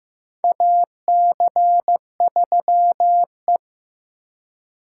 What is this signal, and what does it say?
Morse code, heard as a keyed tone, about 15 wpm, 700 Hz. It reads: AC3E
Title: USA Call Signs - 4x Character Spacing - 15wpm